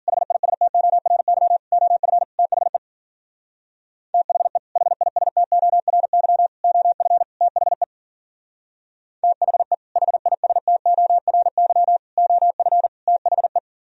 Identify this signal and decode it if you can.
Morse code, heard as a keyed tone, about 52 wpm, 700 Hz. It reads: HISTORY OF THE THE HISTORY OF THE THE HISTORY OF THE